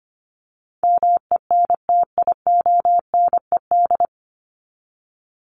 Morse code, heard as a keyed tone, about 25 wpm, 700 Hz. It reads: MENTIONED
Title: ED Suffix - Top 100 Words with Base Words 25wpm